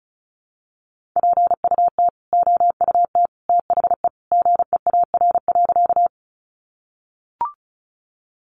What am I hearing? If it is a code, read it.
Morse code, heard as a keyed tone, about 34 wpm, 700 Hz. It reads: PUT OUT THE GEAR.
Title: Sentences from Top 1400 Words 35wpm